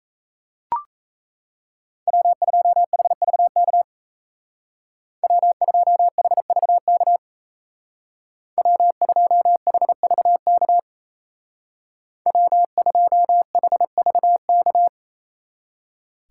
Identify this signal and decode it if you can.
Morse code, heard as a keyed tone, about 34 wpm, 700 Hz. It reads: W2HVK W2HVK W2HVK W2HVK
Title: Speed Racing - USA 1x3 Call Signs 28wpm